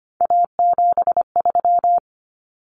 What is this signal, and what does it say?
Morse code, heard as a keyed tone, about 25 wpm, 700 Hz. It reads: A73
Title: Common Cut Numbers - 3-Digit 25wpm